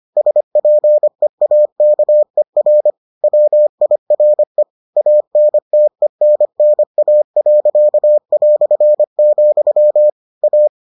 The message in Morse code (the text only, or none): SPEAKER WIRE ANTENNA.", A